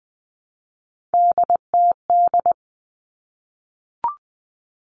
Morse code, heard as a keyed tone, about 20 wpm, 700 Hz. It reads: DTD
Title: Three Challenging Letter-Number Combinations 20wpm